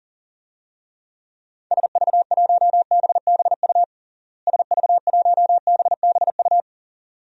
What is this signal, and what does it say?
Morse code, heard as a keyed tone, about 40 wpm, 700 Hz. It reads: SV1BBU SV1BBU
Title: DXCC-100 Call Signs - 6 Characters - Repeated 2x - 40wpm